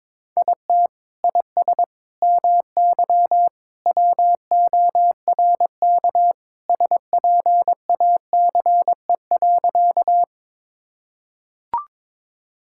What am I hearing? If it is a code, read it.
Morse code, heard as a keyed tone, about 22 wpm, 700 Hz. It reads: IT IS MY WORK SPACE.
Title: Sentences from Top 1000 Words - Review Part 5 - 22wpm